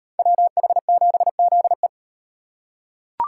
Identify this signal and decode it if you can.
Morse code, heard as a keyed tone, about 38 wpm, 700 Hz. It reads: WH7ZE